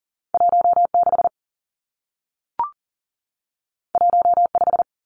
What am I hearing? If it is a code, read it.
Morse code, heard as a keyed tone, about 41 wpm, 700 Hz. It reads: 16 15
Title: Two Challenging Letter-Number Combinations 40wpm